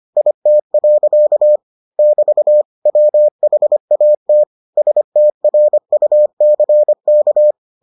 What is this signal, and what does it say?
Morse code, heard as a keyed tone, about 25 wpm, 600 Hz. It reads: IT. <BT> WHAT STRUCK